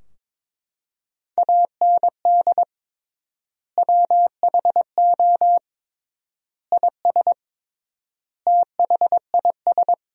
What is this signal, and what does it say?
Morse code, heard as a keyed tone, about 21 wpm, 700 Hz. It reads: AND WHO IS THIS
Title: 3x Word-Spacing - Sentences from Top 300 Words - 22wpm